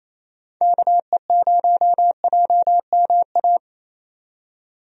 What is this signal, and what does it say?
Morse code, heard as a keyed tone, about 28 wpm, 700 Hz. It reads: KE0JMA
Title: USA Call Signs - 6 Characters - 28wpm